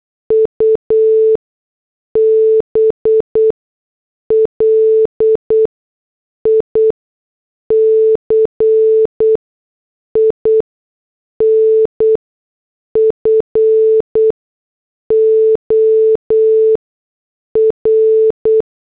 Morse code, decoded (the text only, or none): UBLIC INFOR